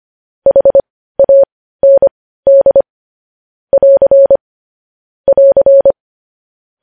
Morse code, heard as a keyed tone, about 25 wpm, spaced as for 15 wpm, 550 Hz. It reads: HAND <AR> <AR>
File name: podcast / News headlines-1x-25x15